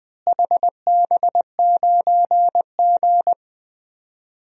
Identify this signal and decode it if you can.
Morse code, heard as a keyed tone, about 21 wpm, 690 Hz. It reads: HB9G
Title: DXCC-100 Call Signs - 4 Characters - Repeated 3x - 20wpm